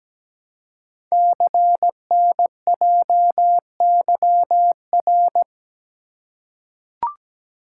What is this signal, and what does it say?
Morse code, heard as a keyed tone, about 17 wpm, 700 Hz. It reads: CNJYR
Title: Code Groups - 5 Letters - 17wpm